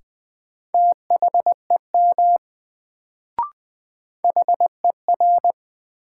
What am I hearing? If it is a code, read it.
Morse code, heard as a keyed tone, about 20 wpm, 700 Hz. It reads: THEM HER
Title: Sets of 2 Words - New Word Drill - 20wpm